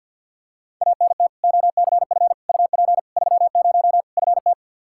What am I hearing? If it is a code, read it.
Morse code, heard as a keyed tone, about 52 wpm, 700 Hz. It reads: ANT OCF UP 30 FT
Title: Practice QSO - Exchange 2 - Info - 50wpm